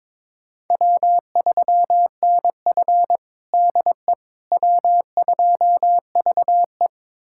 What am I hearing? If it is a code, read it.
Morse code, heard as a keyed tone, about 22 wpm, 700 Hz. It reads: W3NF DE W2VE
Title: USA Callsign DE Callsign - 4 Characters - 22wpm